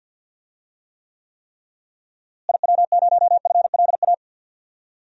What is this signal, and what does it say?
Morse code, heard as a keyed tone, about 51 wpm, 700 Hz. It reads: IW0FLA
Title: DXCC-100 Call Signs - 6 Characters - Repeated 3x - 50wpm